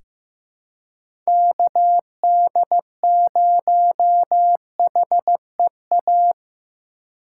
Morse code, heard as a keyed tone, about 15 wpm, 700 Hz. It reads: KD0HEA
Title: USA Call Signs - 6 Characters - 2x Character Spacing - 15wpm